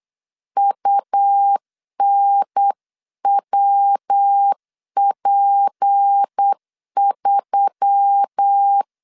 Morse code, heard as a keyed tone, about 8 wpm, 800 Hz. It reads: UNWP3
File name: 8wpm